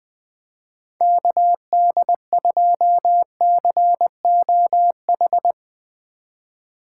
Morse code, heard as a keyed tone, about 20 wpm, 700 Hz. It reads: KD2COH
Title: USA Call Signs - 3x Character Spacing - 20wpm